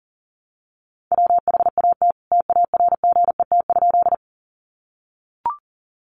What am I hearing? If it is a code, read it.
Morse code, heard as a keyed tone, about 40 wpm, 700 Hz. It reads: WHAT TARGET?
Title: Sentences from Top 1200 Words 40wpm